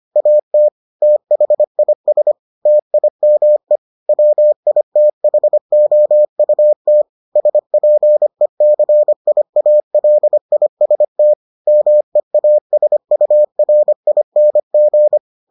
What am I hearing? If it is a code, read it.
Morse code, heard as a keyed tone, about 25 wpm, 600 Hz. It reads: AT THIS TIME WITHOUT SPECIALIST MEASURING